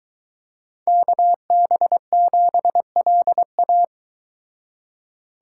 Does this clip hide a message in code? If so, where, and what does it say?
Yes: Morse code from 0.9 to 3.8 s, heard as a keyed tone, about 23 wpm, 700 Hz. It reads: KB7LA